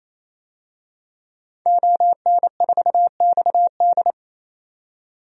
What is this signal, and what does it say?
Morse code, heard as a keyed tone, about 28 wpm, 700 Hz. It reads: ON4XD
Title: DXCC-100 Call Signs - 5 Characters - 28wpm